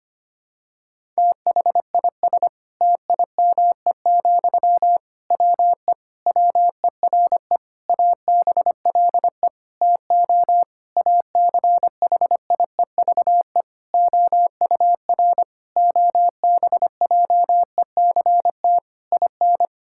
Morse code, heard as a keyed tone, about 25 wpm, 700 Hz. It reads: THIS TIME, WE WERE ABLE TO ACHIEVE OUR OBJECT IN
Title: Sentences from Top 1000 Words - Review Part 6 - 25wpm